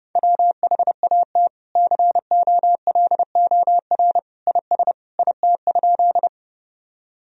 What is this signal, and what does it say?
Morse code, heard as a keyed tone, about 30 wpm, 700 Hz. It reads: WHAT COLOR IS IT?
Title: Sentences from Top 1000 Words - Review Part 2 - 30wpm